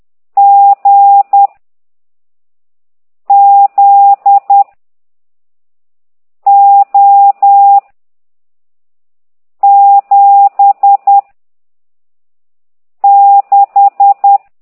Morse code, heard as a keyed tone, about 10 wpm, 800 Hz. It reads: G Z O 7 6
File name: ct2idw / exerc15